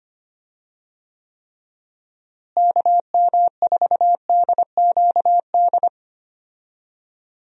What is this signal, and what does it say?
Morse code, heard as a keyed tone, about 25 wpm, 700 Hz. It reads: KM4DQD
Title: USA Call Signs - 6 Characters - Repeated 3x - 25wpm